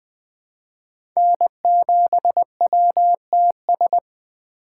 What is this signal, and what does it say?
Morse code, heard as a keyed tone, about 20 wpm, 700 Hz. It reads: N7WTS
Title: USA Call Signs - 5 Characters - 2x Character Spacing - 20wpm